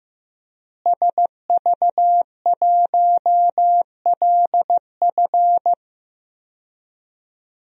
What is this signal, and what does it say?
Morse code, heard as a keyed tone, about 15 wpm, 700 Hz. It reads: SV1LF
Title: DXCC-100 Call Signs - 5 Characters - Repeated 3x - 15wpm